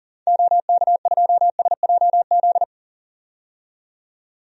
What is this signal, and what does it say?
Morse code, heard as a keyed tone, about 40 wpm, 700 Hz. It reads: OK2SJZ
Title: DXCC-100 Call Signs - 6 Characters - 40wpm